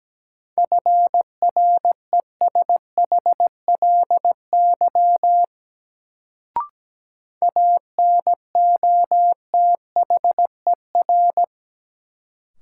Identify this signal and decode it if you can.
Morse code, heard as a keyed tone, about 17 wpm, 700 Hz. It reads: FRESHLY ANOTHER
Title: Sets of 5 Words Encore - New Word Drill - 17wpm